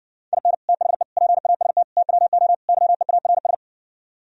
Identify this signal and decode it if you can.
Morse code, heard as a keyed tone, about 69 wpm, 700 Hz. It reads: IN THE PAST TWO YEARS